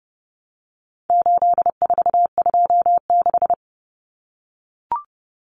Morse code, heard as a keyed tone, about 30 wpm, 700 Hz. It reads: 8426
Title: Numbers - 4-Digit 30wpm